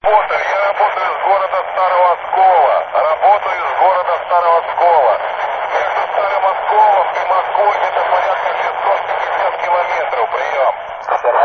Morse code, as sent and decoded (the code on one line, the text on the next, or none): none